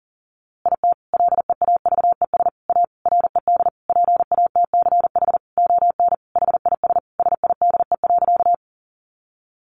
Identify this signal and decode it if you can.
Morse code, heard as a keyed tone, about 40 wpm, 700 Hz. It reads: IT LEAVES A RED PATCH ON HIS SIDE.
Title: A Sound Of Thunder 40wpm